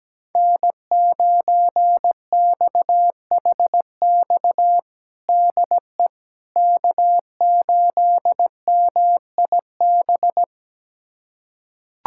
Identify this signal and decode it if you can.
Morse code, heard as a keyed tone, about 17 wpm, 700 Hz. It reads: N9XHX DE K8MIB
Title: USA Callsign DE Callsign - 5 Characters - 17wpm